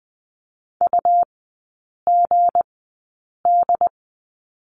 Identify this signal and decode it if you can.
Morse code, heard as a keyed tone, about 20 wpm, spaced as for 8 wpm, 700 Hz. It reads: UGD
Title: 166 Lesson 27e - Number 7 - Three Characters 20wpm